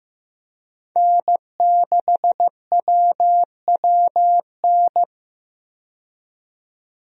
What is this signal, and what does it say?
Morse code, heard as a keyed tone, about 15 wpm, 700 Hz. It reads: N6WWN
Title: USA Call Signs - 5 Characters - 15wpm